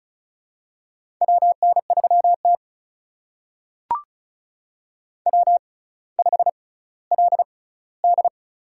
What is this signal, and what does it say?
Morse code, heard as a keyed tone, about 36 wpm, spaced as for 18 wpm, 700 Hz. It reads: WN3T W5LD
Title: USA Call Signs - 4 Characters - 6x Character Spacing - 35wpm